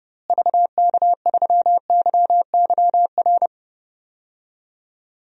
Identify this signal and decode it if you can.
Morse code, heard as a keyed tone, about 30 wpm, 700 Hz. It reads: VK3YYR
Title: DXCC-100 Call Signs - 6 Characters - Repeated 2x - 30wpm